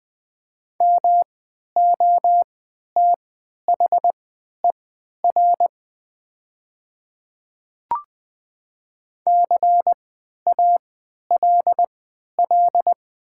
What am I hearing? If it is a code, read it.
Morse code, heard as a keyed tone, about 20 wpm, spaced as for 11 wpm, 700 Hz. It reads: MOTHER CALL
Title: 3x Character Spacing - Top 200 Words - 20wpm